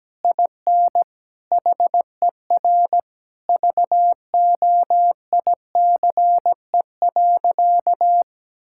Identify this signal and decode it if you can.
Morse code, heard as a keyed tone, about 17 wpm, 700 Hz. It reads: IN HER VOICE.
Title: Sentences from Top 1000 Words - Review Part 6 - 17wpm